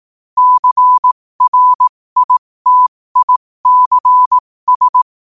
Morse code, heard as a keyed tone, about 18 wpm, spaced as for 15 wpm, 1000 Hz. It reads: CRITICS
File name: qotdfiles / 15wpm